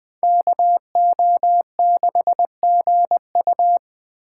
Morse code, heard as a keyed tone, about 24 wpm, 700 Hz. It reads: KO6GU
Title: Speed Racing - USA 2x2 Call Signs 20wpm